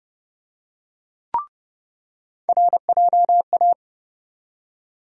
Morse code, heard as a keyed tone, about 32 wpm, 700 Hz. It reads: RJA
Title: USA Call Sign Postfixes - 3 Letters 30wpm